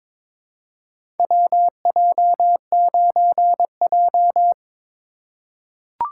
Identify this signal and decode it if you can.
Morse code, heard as a keyed tone, about 22 wpm, 700 Hz. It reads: WJ9J
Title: USA Call Signs - 4 Characters - 2x Character Spacing - 22wpm